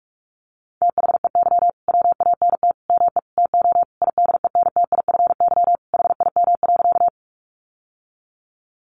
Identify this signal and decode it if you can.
Morse code, heard as a keyed tone, about 44 wpm, 700 Hz. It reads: THEY WANT ME TO IDENTIFY HIM.
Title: Sentences from Top 900 Words 45wpm